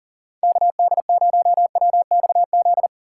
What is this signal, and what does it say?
Morse code, heard as a keyed tone, about 39 wpm, 700 Hz. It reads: KD0WXZ
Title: USA Call Signs - 6 Characters - 3x Character Spacing - 40wpm